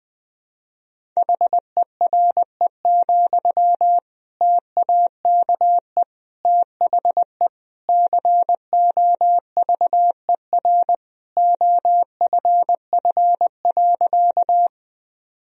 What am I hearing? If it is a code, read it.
Morse code, heard as a keyed tone, about 20 wpm, 700 Hz. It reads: HERE, TAKE THE COVER OFF.
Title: Sentences from Top 1000 Words - Review Part 1 - 20wpm